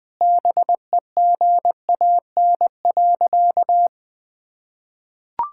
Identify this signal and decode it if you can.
Morse code, heard as a keyed tone, about 20 wpm, 700 Hz. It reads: BEGAN.
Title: Sentences from Top 1000 Words - Review Part 5 - 20wpm